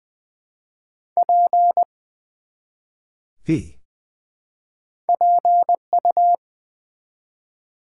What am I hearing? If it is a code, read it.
Morse code, heard as a keyed tone, about 20 wpm, 700 Hz. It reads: P PU
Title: Build A Word - 1 Letter At A Time - 3 to 5-Letter Words - 20wpm